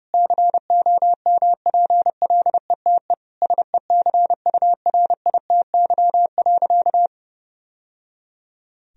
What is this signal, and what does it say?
Morse code, heard as a keyed tone, about 30 wpm, 700 Hz. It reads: COMPLETE SECURITY.
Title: Sentences from Top 1000 Words - Review Part 4 - 30wpm